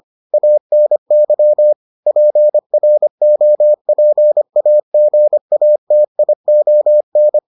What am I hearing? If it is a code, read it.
Morse code, heard as a keyed tone, about 25 wpm, 600 Hz. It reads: ANY PROPAGATION